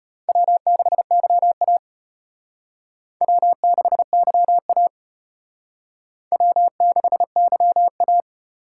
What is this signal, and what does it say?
Morse code, heard as a keyed tone, about 32 wpm, 700 Hz. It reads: W6YA W6YA W6YA